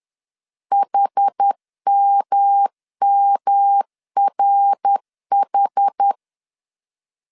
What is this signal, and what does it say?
Morse code, heard as a keyed tone, about 10 wpm, 800 Hz. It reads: HMMRH
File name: CW / DET / 1 3 1500C S B 10wpm 3335